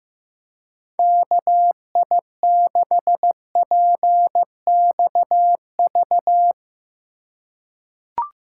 Morse code, heard as a keyed tone, about 15 wpm, 700 Hz. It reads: KI6PXV
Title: USA Call Signs - 6 Characters - 3x Character Spacing - 15wpm